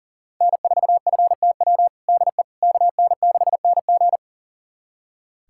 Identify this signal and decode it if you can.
Morse code, heard as a keyed tone, about 37 wpm, 700 Hz. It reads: N4FTW DE KN6NG
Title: USA Callsign DE Callsign - 5 Characters - 40wpm